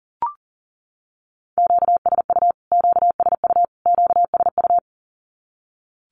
Morse code, heard as a keyed tone, about 40 wpm, 700 Hz. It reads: QSU QSU QSU
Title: Q-Codes Repeated 3x 40wpm